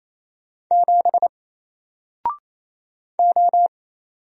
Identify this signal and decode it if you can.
Morse code, heard as a keyed tone, about 28 wpm, 700 Hz. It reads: 7 O
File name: Single-Letter-Number-28wpm